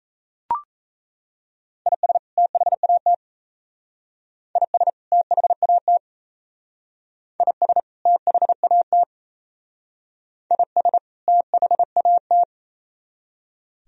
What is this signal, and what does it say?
Morse code, heard as a keyed tone, about 36 wpm, 700 Hz. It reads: IS THAT IS THAT IS THAT IS THAT